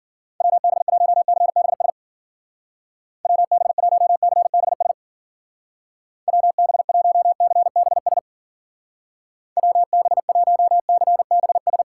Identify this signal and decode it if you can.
Morse code, heard as a keyed tone, about 52 wpm, 700 Hz. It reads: WB1CBS WB1CBS WB1CBS WB1CBS